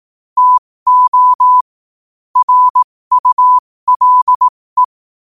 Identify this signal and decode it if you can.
Morse code, heard as a keyed tone, about 18 wpm, spaced as for 15 wpm, 1000 Hz. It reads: TO RULE